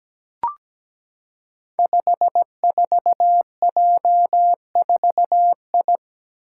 Morse code, heard as a keyed tone, about 17 wpm, 700 Hz. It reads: 54J4I
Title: Five Challenging Letter-Number Combinations 17wpm